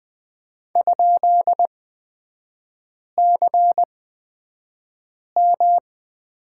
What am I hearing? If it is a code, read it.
Morse code, heard as a keyed tone, about 20 wpm, spaced as for 5 wpm, 700 Hz. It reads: ?CM
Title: 250 Lesson 40e - BK - Three Characters 20wpm